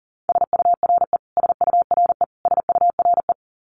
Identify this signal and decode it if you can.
Morse code, heard as a keyed tone, about 40 wpm, 700 Hz. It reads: SURE SURE SURE
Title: Top 500 Words Repeated 40wpm